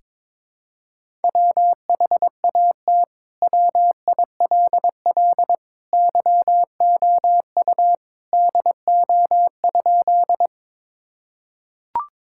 Morse code, heard as a keyed tone, about 22 wpm, 700 Hz. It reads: WHAT WILL YOU DO?
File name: practice / 22wpm / Sentences with 2-Word Phrases - 22wpm